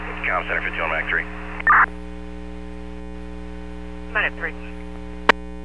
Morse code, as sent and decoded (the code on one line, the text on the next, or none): none